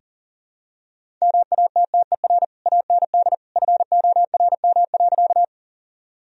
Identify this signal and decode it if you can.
Morse code, heard as a keyed tone, about 40 wpm, 700 Hz. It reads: MATTER AND FORM.
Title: Sentences from Top 1000 Words - Review Part 1 - 40wpm